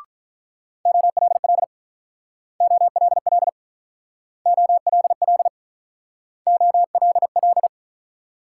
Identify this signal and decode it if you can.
Morse code, heard as a keyed tone, about 44 wpm, 690 Hz. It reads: OLL OLL OLL OLL